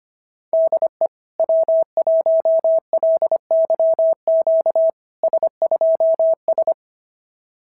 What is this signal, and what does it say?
Morse code, heard as a keyed tone, about 23 wpm, 650 Hz. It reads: DE W1LYQ S2S
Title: SOTA QSO Practice 25wpm